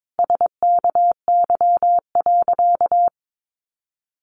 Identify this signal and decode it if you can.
Morse code, heard as a keyed tone, about 22 wpm, 700 Hz. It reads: SKY.
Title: Sentences from Top 1200 Words 22wpm